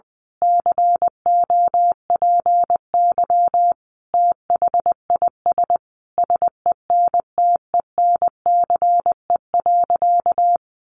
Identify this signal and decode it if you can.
Morse code, heard as a keyed tone, about 20 wpm, 700 Hz. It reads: COPY THIS SENTENCE.
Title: Additional Easy Phrases 20wpm